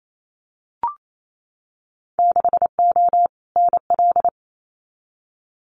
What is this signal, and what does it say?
Morse code, heard as a keyed tone, about 28 wpm, 700 Hz. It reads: 6O NL
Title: Winter Field Day Exchanges Encore 28wpm